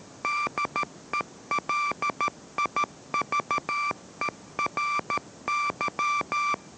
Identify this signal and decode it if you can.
Morse code, heard as a keyed tone, about 13 wpm, 1170 Hz, with noise. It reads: DELIVERY